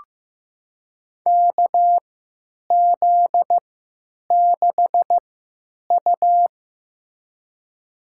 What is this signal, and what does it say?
Morse code, heard as a keyed tone, about 16 wpm, spaced as for 9 wpm, 700 Hz. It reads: KZ6U
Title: USA Call Signs - 4 Characters - 3x Character Spacing - 15wpm